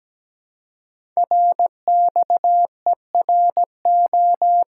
Morse code, heard as a keyed tone, about 17 wpm, 700 Hz. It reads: RXERO
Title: Code Groups - 5 Letters - 17wpm